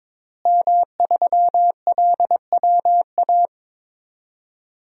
Morse code, heard as a keyed tone, about 22 wpm, 700 Hz. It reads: M3LWA
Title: DXCC-100 Call Signs - 5 Characters - Repeated 3x - 22wpm